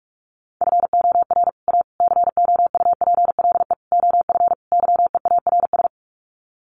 Morse code, heard as a keyed tone, about 44 wpm, 700 Hz. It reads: FOR A COUPLE OF YEARS
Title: Sets of 5 Words Encore 45wpm